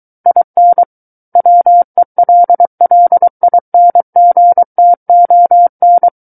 Morse code, heard as a keyed tone, about 23 wpm, 700 Hz. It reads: IN WELLINGTON
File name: Book One / 14-Chapter 14 - 23 WPM